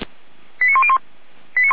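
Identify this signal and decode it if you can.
Five-tone selective call (ZVEI), heard as quick runs of five tones, tones of about 70 ms. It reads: 89101
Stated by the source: Five-tone ZVEI selective call received on an SDR.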